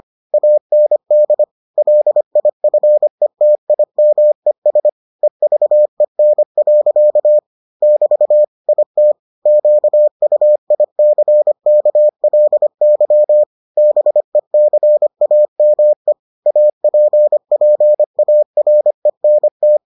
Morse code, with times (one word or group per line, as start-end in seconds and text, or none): 0.3-1.4 AND
1.8-4.9 LIFETIMES
5.2-7.4 EVEN.
7.8-8.4 <BT>
8.7-9.1 IT
9.5-13.4 QUICKLY
13.8-16.1 BECAME
16.5-19.8 APPARENT